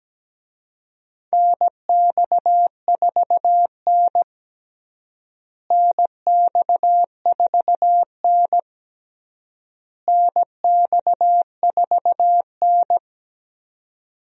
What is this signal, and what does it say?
Morse code, heard as a keyed tone, about 17 wpm, 700 Hz. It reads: NX4N NX4N NX4N
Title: USA Call Signs - 4 Characters - Repeated 3x - 17wpm